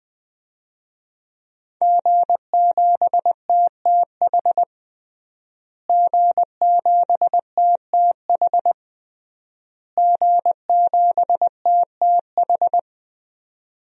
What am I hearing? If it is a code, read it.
Morse code, heard as a keyed tone, about 20 wpm, 700 Hz. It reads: G7TTH G7TTH G7TTH
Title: DXCC-100 Call Signs - 5 Characters - Repeated 3x - 20wpm